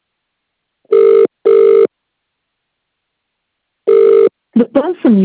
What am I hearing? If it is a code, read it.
Ringback tone (UK), 400 + 450 Hz, with speech.